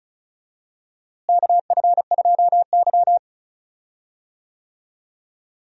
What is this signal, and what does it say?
Morse code, heard as a keyed tone, about 34 wpm, 700 Hz. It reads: KF2Y
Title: USA Call Signs - 4 Characters - Repeated 2x - 35wpm